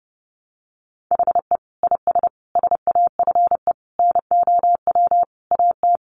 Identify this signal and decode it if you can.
Morse code, heard as a keyed tone, about 30 wpm, 700 Hz. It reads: HE IS SAFE NOW AT
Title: Sentences from Top 900 Words 30wpm